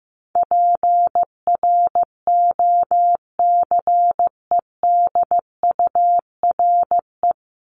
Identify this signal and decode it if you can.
Morse code, heard as a keyed tone, about 15 wpm, 700 Hz. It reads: PROCEDURE
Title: Top 1751-2000 Words 15wpm